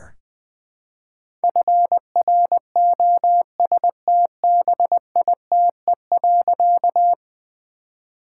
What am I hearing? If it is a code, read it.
Morse code, heard as a keyed tone, about 20 wpm, 700 Hz. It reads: FROSTBITE.
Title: All-Age Appropriate Jokes - 20wpm